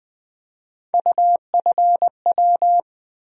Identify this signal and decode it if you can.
Morse code, heard as a keyed tone, about 21 wpm, 700 Hz. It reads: UFW
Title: USA Call Sign Postfixes - 3 Letters 20wpm